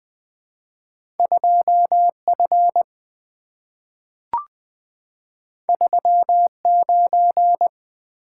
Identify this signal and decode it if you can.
Morse code, heard as a keyed tone, about 20 wpm, 700 Hz. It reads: 2F 39
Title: Two Letter-Number Combinations - 20wpm